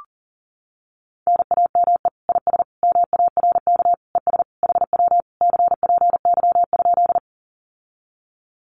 Morse code, heard as a keyed tone, about 40 wpm, 700 Hz. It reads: NAME IS MARK ES HW CPY?
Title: Common QSO Phrases Encore 40wpm